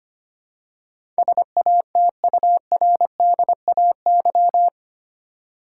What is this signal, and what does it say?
Morse code, heard as a keyed tone, about 25 wpm, 700 Hz. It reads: SATURDAY
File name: Eight Letter Words - 25wpm